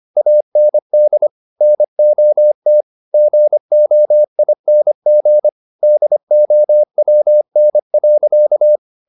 Morse code, heard as a keyed tone, about 25 wpm, 600 Hz. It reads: AND NOT GOING DOWN.
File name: the old man and the sea part 1 25wpm